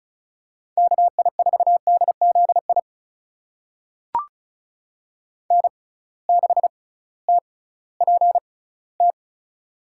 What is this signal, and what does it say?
Morse code, heard as a keyed tone, about 36 wpm, spaced as for 19 wpm, 700 Hz. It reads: KI4DZI N6TPT
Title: USA Call Signs - 6x Character Spacing - 35wpm